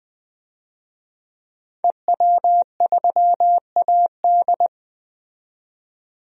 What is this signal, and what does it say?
Morse code, heard as a keyed tone, about 20 wpm, 700 Hz. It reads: EW3AD
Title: DXCC-100 Call Signs - 5 Characters - 20wpm